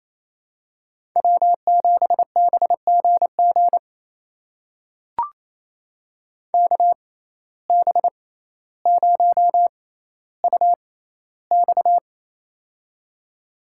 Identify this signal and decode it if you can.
Morse code, heard as a keyed tone, about 28 wpm, spaced as for 14 wpm, 700 Hz. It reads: W7BGG KB0UX
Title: USA Call Signs - 5 Characters - 6x Character Spacing - 28wpm